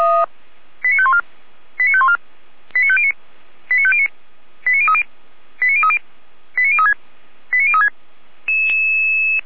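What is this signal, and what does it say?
Five-tone selective call (ZVEI), heard as quick runs of five tones, tones of about 70 ms. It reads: 89514 89514 89509 89509 89030 89030 89036 89036
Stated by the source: ZVEI five-tone call 89011 received on an SDR.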